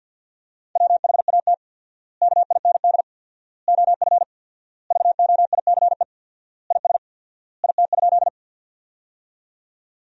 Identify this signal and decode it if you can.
Morse code, heard as a keyed tone, about 48 wpm, 700 Hz. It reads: WHAT KIND OF VOICE IS IT?
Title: 4x Word-Spacing - Sentences from Top 500 Words - 50wpm